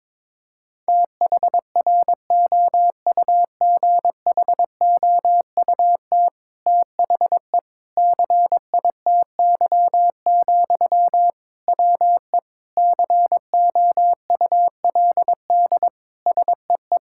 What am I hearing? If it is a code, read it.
Morse code, heard as a keyed tone, about 21 wpm, 700 Hz. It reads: THROUGHOUT THE CITY, WE COULD SEE
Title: Sentences from Top 1000 Words - Review Part 5 - 22wpm